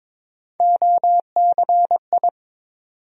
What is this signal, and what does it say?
Morse code, heard as a keyed tone, about 22 wpm, 700 Hz. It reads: OCI
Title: USA Call Sign Postfixes - 3 Letters 22wpm